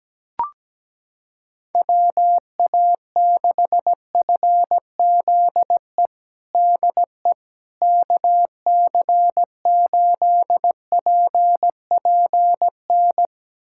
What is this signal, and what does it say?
Morse code, heard as a keyed tone, about 17 wpm, 700 Hz. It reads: WA6FZE DE KC8PPN
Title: USA Callsign DE Callsign - 6 Characters - 17wpm